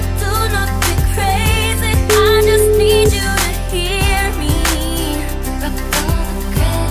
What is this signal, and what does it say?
Ringback tone (Europe, 425 Hz), with noise.